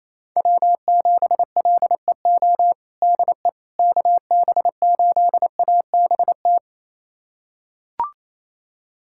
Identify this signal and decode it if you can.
Morse code, heard as a keyed tone, about 28 wpm, 700 Hz. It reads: W7LEO DE KB8ABT
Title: USA Callsign DE Callsign 28wpm